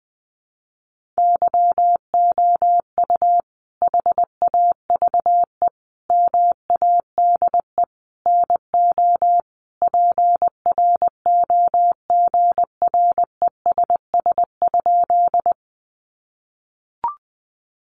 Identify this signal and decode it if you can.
Morse code, heard as a keyed tone, about 20 wpm, 700 Hz. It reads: YOU HAVE MADE NO PROGRESS?
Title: Sentences from Top 1400 Words 20wpm